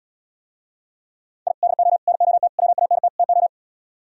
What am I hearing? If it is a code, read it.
Morse code, heard as a keyed tone, about 77 wpm, 700 Hz. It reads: I DO NOT WANT TO